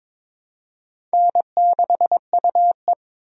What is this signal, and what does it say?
Morse code, heard as a keyed tone, about 22 wpm, 700 Hz. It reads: N6UE
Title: USA Call Signs - 4 Characters - 22wpm